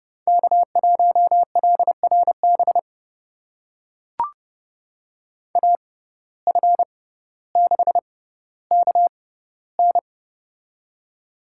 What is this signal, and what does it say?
Morse code, heard as a keyed tone, about 32 wpm, spaced as for 16 wpm, 700 Hz. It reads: K1LRB AF6KN
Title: USA Call Signs - 5 Characters - 6x Character Spacing - 30wpm